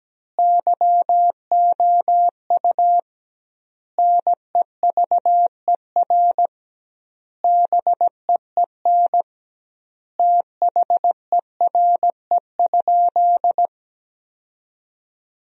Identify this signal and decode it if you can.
Morse code, heard as a keyed tone, about 17 wpm, 700 Hz. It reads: YOU NEVER BEEN THERE?
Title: 2x Word-Spacing - Sentences from Top 300 Words - 17wpm